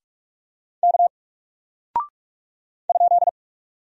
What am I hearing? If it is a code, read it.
Morse code, heard as a keyed tone, about 44 wpm, 700 Hz. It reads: K ?
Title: Single Character (Letter:Number:Punctuation:Prosign) Alt 45wpm